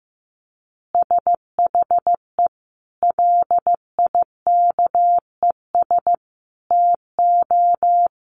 Morse code, heard as a keyed tone, about 15 wpm, 700 Hz. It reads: SHE LIKES TO
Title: Easy Phrases 15wpm